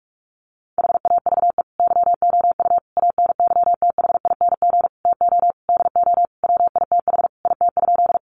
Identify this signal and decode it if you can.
Morse code, heard as a keyed tone, about 44 wpm, 700 Hz. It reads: HAVE YOU ANYTHING TO DO WITH IT?
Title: Sentences from Top 1000 Words 45wpm